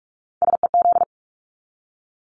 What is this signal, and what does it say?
Morse code, heard as a keyed tone, about 44 wpm, 700 Hz. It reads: SEZ
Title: Abbreviations 45wpm